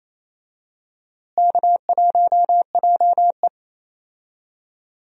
Morse code, heard as a keyed tone, about 28 wpm, 700 Hz. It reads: K1JE
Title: USA Call Signs - 4 Characters - Repeated 3x - 28wpm